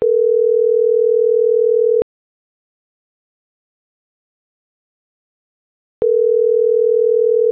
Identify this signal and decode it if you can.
Ringback tone (North America), 440 + 480 Hz.